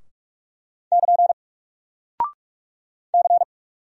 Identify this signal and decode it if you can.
Morse code, heard as a keyed tone, about 46 wpm, 700 Hz. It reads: <KN> C
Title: Single Character (Letter:Number:Punctuation:Prosign) Alt 45wpm